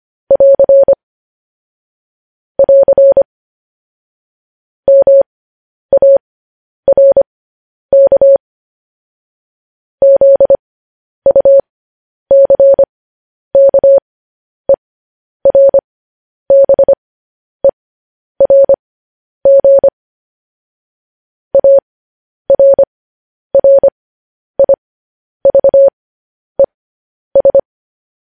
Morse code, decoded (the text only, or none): <AR> <AR> MARK ZUCKERBERG ARRIVES